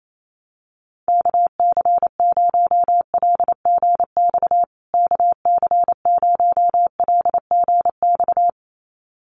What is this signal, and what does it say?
Morse code, heard as a keyed tone, about 28 wpm, 700 Hz. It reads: KC0LGX KC0LGX
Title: USA Call Signs Repeated 2x 28wpm